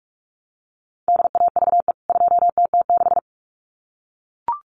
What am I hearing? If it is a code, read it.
Morse code, heard as a keyed tone, about 44 wpm, 700 Hz. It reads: DAVE 2TT6
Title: CWT Exchange 45wpm